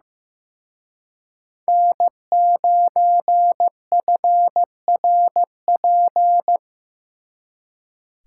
Morse code, heard as a keyed tone, about 18 wpm, 700 Hz. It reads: N9FRP
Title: Speed Racing - USA 1x3 Call Signs 15wpm